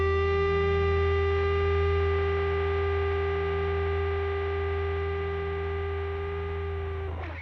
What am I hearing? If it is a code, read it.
Dial tone (Japan), 400 Hz, with noise.